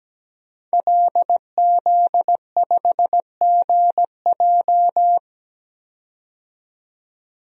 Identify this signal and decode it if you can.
Morse code, heard as a keyed tone, about 17 wpm, 700 Hz. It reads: LZ5GJ
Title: DXCC-100 Call Signs - 5 Characters - 17wpm